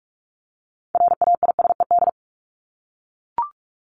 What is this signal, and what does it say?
Morse code, heard as a keyed tone, about 44 wpm, 700 Hz. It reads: RAISED
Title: ED Suffix - Top 100 Words with Base Words 45wpm